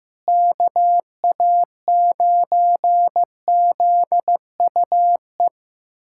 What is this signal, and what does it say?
Morse code, heard as a keyed tone, about 16 wpm, 700 Hz. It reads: KA9ZUE
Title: USA Callsign DE Callsign 15wpm